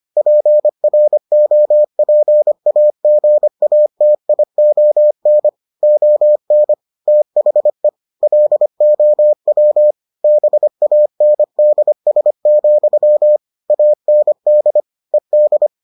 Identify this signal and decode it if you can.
Morse code, heard as a keyed tone, about 25 wpm, 600 Hz. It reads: PROPAGATION ON THE LOW BANDS, AND ED